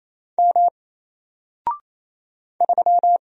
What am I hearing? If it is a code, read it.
Morse code, heard as a keyed tone, about 28 wpm, 700 Hz. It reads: M 3
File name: Single-Letter-Number-28wpm